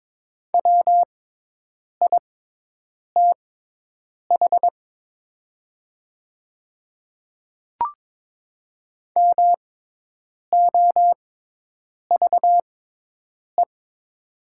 Morse code, heard as a keyed tone, about 22 wpm, spaced as for 8 wpm, 690 Hz. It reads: WITH MOVE
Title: 6x Character Spacing - Top 500 Words - 22wpm